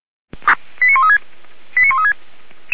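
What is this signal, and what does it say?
Five-tone selective call (ZVEI), heard as quick runs of five tones, tones of about 70 ms. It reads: 89136 89136
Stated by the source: ZVEI five-tone selective call received on an SDR.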